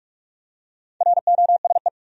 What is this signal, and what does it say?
Morse code, heard as a keyed tone, about 46 wpm, 700 Hz. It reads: ROSE
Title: USA Common Names - Max 4 Letters - 45wpm